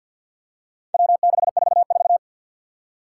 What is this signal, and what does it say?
Morse code, heard as a keyed tone, about 51 wpm, 700 Hz. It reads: W644
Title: Four Challenging Letter-Number Combinations 50wpm